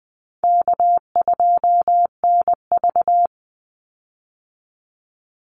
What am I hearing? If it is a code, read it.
Morse code, heard as a keyed tone, about 20 wpm, 700 Hz. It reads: K2NV
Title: Easy Call Signs - 20wpm